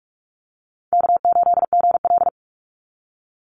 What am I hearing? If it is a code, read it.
Morse code, heard as a keyed tone, about 44 wpm, 700 Hz. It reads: K8GL
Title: Easy Call Signs - 45wpm